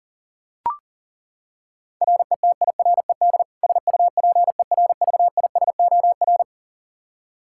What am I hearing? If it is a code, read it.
Morse code, heard as a keyed tone, about 40 wpm, 700 Hz. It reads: RETIRED SUPERVISOR
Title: K1USN SST QSO Practice 40wpm